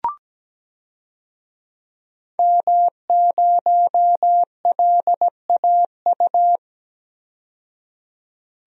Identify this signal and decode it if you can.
Morse code, heard as a keyed tone, about 17 wpm, 700 Hz. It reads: M0LAU
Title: DXCC-100 Call Signs - 5 Characters - 17wpm